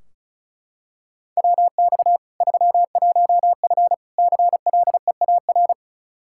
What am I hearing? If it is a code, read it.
Morse code, heard as a keyed tone, about 36 wpm, 700 Hz. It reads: WX 31F CLEAR
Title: Practice QSO - Exchange 2 - Info - 35wpm